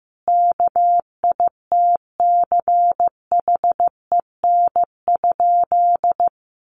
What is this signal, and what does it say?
Morse code, heard as a keyed tone, about 15 wpm, 700 Hz. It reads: KITCHEN?
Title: Sentences from Top 1400 Words 15wpm